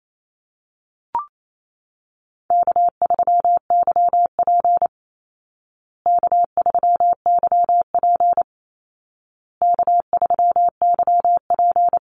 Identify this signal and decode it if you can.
Morse code, heard as a keyed tone, about 28 wpm, 700 Hz. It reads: K3YP K3YP K3YP
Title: Call Signs Repeated 2x - 28wpm